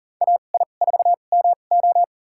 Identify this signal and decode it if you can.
Morse code, heard as a keyed tone, about 40 wpm, spaced as for 29 wpm, 700 Hz. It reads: AI4MO
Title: USA Call Signs - 5 Characters - 2x Character Spacing - 40wpm